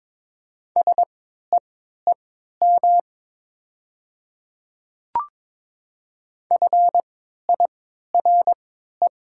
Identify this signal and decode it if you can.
Morse code, heard as a keyed tone, about 22 wpm, spaced as for 12 wpm, 700 Hz. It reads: SEEM FIRE
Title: 3x Character Spacing - Top 500 Words - 22wpm